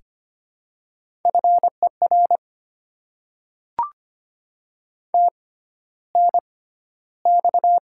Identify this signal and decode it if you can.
Morse code, heard as a keyed tone, about 25 wpm, spaced as for 13 wpm, 700 Hz. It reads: FER TNX
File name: Top 20 QSO Elements - 6x Character Spacing - 25wpm